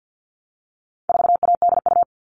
Morse code, heard as a keyed tone, about 50 wpm, 700 Hz. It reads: 4ADU
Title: All Cut Numbers - 4-Digit 50wpm